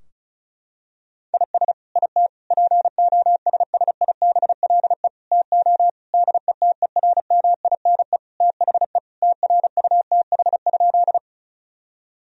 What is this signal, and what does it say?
Morse code, heard as a keyed tone, about 36 wpm, 700 Hz. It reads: IS IT POSSIBLE TO DETERMINE THE TRUTH?
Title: Sentences from Top 1000 Words - Review Part 4 - 35wpm